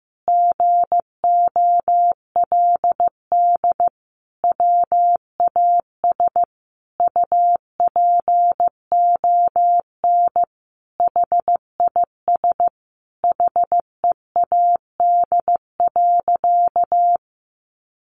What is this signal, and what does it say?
Morse code, heard as a keyed tone, about 15 wpm, 700 Hz. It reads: GOLD WAS UPON HIS HEAD.
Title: Sentences from Top 1100 Words 15wpm